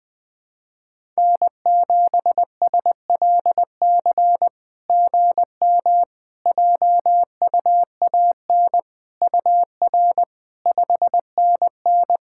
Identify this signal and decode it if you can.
Morse code, heard as a keyed tone, about 21 wpm, 700 Hz. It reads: N7SLC GM JUAN UR 5NN
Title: SOTA QSO Practice 20wpm